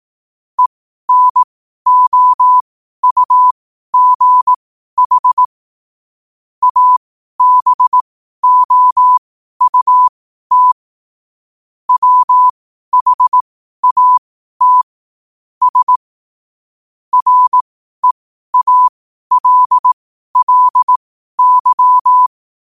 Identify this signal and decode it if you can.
Morse code, heard as a keyed tone, about 18 wpm, spaced as for 12 wpm, 1000 Hz. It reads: ENOUGH ABOUT WHAT S REALLY
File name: qotdfiles / 12wpm